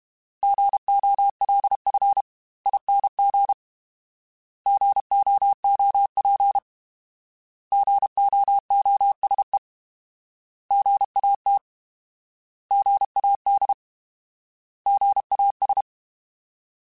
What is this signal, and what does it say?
Morse code, heard as a keyed tone, about 32 wpm, 800 Hz. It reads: GOLFING GOOP GOOSE GAT GAD GAS
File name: Alphabetical / Alphabetical mp3s / Gs @30